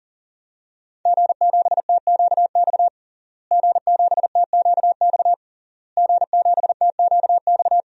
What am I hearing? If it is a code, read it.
Morse code, heard as a keyed tone, about 37 wpm, 700 Hz. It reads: G7TQX G7TQX G7TQX
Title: DXCC-100 Call Signs - 5 Characters - Repeated 3x - 40wpm